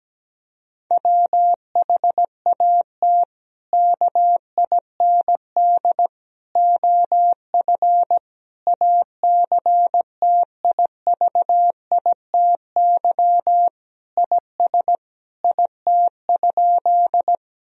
Morse code, heard as a keyed tone, about 17 wpm, 700 Hz. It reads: WHAT KIND OF ACTIVITY IS IT?
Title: Sentences from Top 1000 Words - Review Part 4 - 17wpm